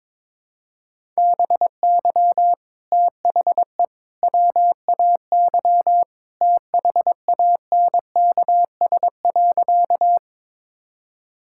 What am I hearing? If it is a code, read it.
Morse code, heard as a keyed tone, about 22 wpm, 700 Hz. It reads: BY THE WAY THANKS.
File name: practice / 22wpm / Sentences with 3-Word Phrases - 22wpm